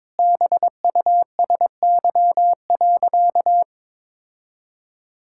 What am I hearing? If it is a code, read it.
Morse code, heard as a keyed tone, about 22 wpm, 700 Hz. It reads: BUSY.
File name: practice / 22wpm / Sentences with 3-Word Phrases - 22wpm